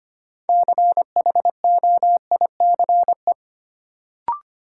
Morse code, heard as a keyed tone, about 25 wpm, 700 Hz. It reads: CHOICE
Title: Sets of 5 Words - New Word Drill - 25wpm